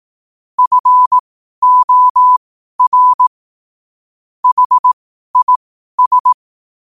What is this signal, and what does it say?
Morse code, heard as a keyed tone, about 18 wpm, spaced as for 12 wpm, 1000 Hz. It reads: FOR HIS
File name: qotdfiles / 12wpm